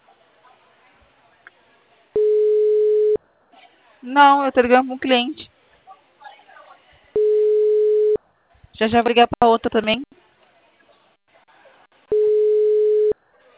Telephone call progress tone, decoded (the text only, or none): ringback tone